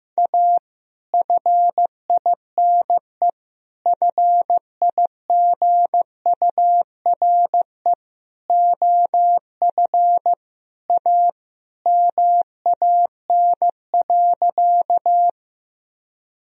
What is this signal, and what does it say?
Morse code, heard as a keyed tone, about 15 wpm, 700 Hz. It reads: A FINE FIGURE OF A MAN.
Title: Sentences from Top 1000 Words - Review Part 2 - 15wpm